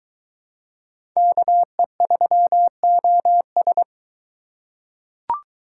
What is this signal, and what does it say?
Morse code, heard as a keyed tone, about 20 wpm, 700 Hz. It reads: KE3OS